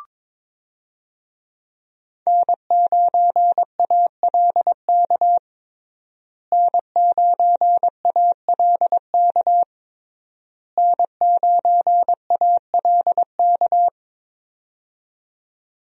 Morse code, heard as a keyed tone, about 22 wpm, 700 Hz. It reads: N9ALK N9ALK N9ALK
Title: USA Call Signs - 5 Characters - Repeated 3x - 22wpm